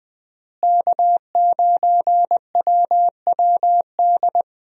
Morse code, heard as a keyed tone, about 20 wpm, 700 Hz. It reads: K9WWD
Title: USA Call Signs - 5 Characters - Repeated 3x - 20wpm